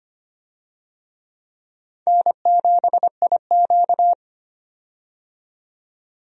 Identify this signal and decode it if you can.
Morse code, heard as a keyed tone, about 25 wpm, 700 Hz. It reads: N7IQ
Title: USA Call Signs - 4 Characters - 25wpm